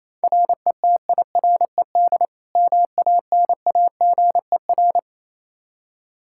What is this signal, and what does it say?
Morse code, heard as a keyed tone, about 28 wpm, 700 Hz. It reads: RETIRED MANAGER
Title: K1USN SST QSO Practice 28wpm